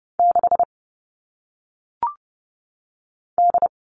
Morse code, heard as a keyed tone, about 30 wpm, 700 Hz. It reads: 6 D
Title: All Cut Numbers - 1-Digit 30wpm